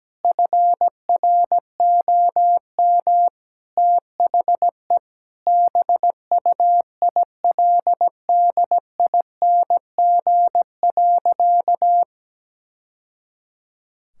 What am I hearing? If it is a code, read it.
Morse code, heard as a keyed tone, about 17 wpm, 700 Hz. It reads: FROM THE BUILDING.
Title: Sentences from Top 1000 Words - Review Part 5 - 17wpm